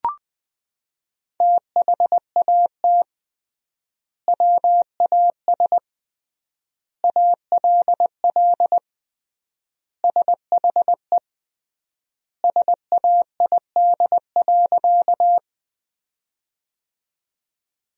Morse code, heard as a keyed tone, about 20 wpm, 700 Hz. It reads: THAT WAS ALL SHE SAID.
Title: 3x Word-Spacing - Sentences from Top 100 Words - 20wpm